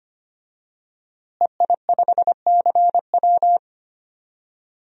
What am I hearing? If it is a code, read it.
Morse code, heard as a keyed tone, about 25 wpm, 700 Hz. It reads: EI5CW
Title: DXCC-100 Call Signs - 5 Characters - Repeated 2x - 25wpm